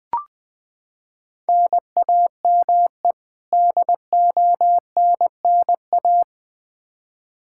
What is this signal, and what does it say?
Morse code, heard as a keyed tone, about 20 wpm, 700 Hz. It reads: NAME DONNA
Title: Practice QSO - Exchange 1 - Beginning - 20wpm